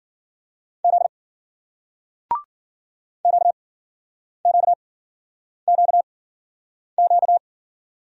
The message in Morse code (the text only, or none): Z QQQQ